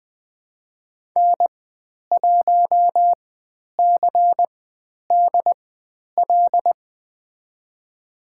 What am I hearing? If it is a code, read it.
Morse code, heard as a keyed tone, about 20 wpm, spaced as for 10 wpm, 700 Hz. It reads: N1CDL
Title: 095 Lesson 16g - Letter c - Call Signs 20wpm